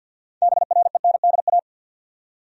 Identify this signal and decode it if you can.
Morse code, heard as a keyed tone, about 46 wpm, 700 Hz. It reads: BRENDA
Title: USA Common Names - 50wpm